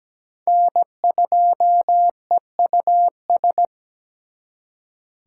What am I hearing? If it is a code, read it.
Morse code, heard as a keyed tone, about 20 wpm, 700 Hz. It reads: N2EUS
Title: Speed Racing - USA 1x3 Call Signs 17wpm